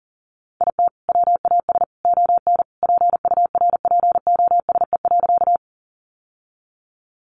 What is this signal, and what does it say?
Morse code, heard as a keyed tone, about 40 wpm, 700 Hz. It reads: IT WAS ON PURPOSE.
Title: Sentences from Top 800 Words 40wpm